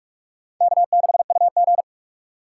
Morse code, heard as a keyed tone, about 44 wpm, 700 Hz. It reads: K6UG